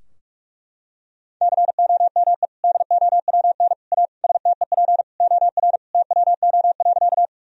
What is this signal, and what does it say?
Morse code, heard as a keyed tone, about 43 wpm, 700 Hz. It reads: COME DOWN A STEP OR TWO.
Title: Sentences from Top 1000 Words - Review Part 2 - 45wpm